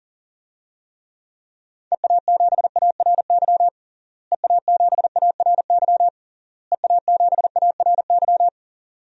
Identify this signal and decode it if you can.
Morse code, heard as a keyed tone, about 40 wpm, 700 Hz. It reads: EA7ARY EA7ARY EA7ARY
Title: DXCC-100 Call Signs - 6 Characters - Repeated 3x - 40wpm